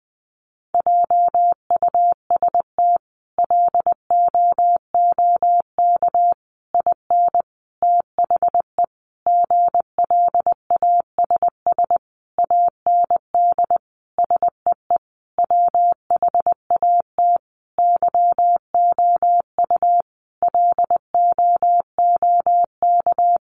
Morse code, heard as a keyed tone, about 20 wpm, 700 Hz. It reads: JUST LOOK IN THE GLASS AND SEE WHAT YOU LOOK
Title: Sentences from Top 1300 Words 20wpm